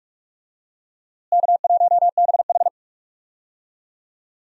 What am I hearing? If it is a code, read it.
Morse code, heard as a keyed tone, about 46 wpm, 700 Hz. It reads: K1BH
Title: USA Call Signs - 4 Characters - Repeated 2x - 45wpm